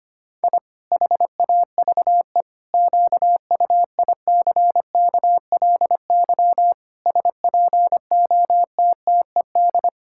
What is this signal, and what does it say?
Morse code, heard as a keyed tone, about 25 wpm, 700 Hz. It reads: I HAVE QUICKLY SPOTTED
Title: Pangram Challenge v2 - 25wpm